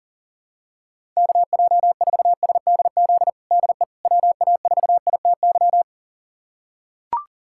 Morse code, heard as a keyed tone, about 40 wpm, 700 Hz. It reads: KJ4SDZ DE WA4ITY
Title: USA Callsign DE Callsign - 6 Characters - 40wpm